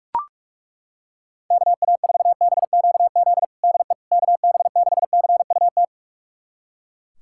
Morse code, heard as a keyed tone, about 42 wpm, 700 Hz. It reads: KA4BQZ DE KB6CUT
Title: USA Callsign DE Callsign - 6 Characters - 45wpm